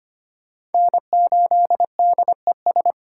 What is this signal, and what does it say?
Morse code, heard as a keyed tone, about 29 wpm, 700 Hz. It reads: N8DES
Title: Speed Racing - USA 1x3 Call Signs 25wpm